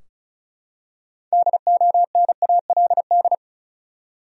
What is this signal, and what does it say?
Morse code, heard as a keyed tone, about 36 wpm, 700 Hz. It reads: DONALD
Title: USA Common Names - 35wpm